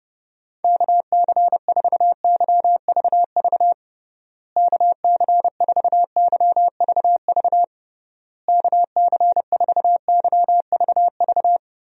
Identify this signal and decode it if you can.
Morse code, heard as a keyed tone, about 30 wpm, 700 Hz. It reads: KC4YVV KC4YVV KC4YVV
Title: USA Call Signs - 6 Characters - Repeated 3x - 30wpm